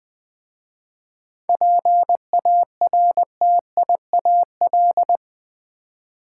Morse code, heard as a keyed tone, about 20 wpm, 700 Hz. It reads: PARTIAL
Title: 7 Letter Words - First 3 Letters Given - 20wpm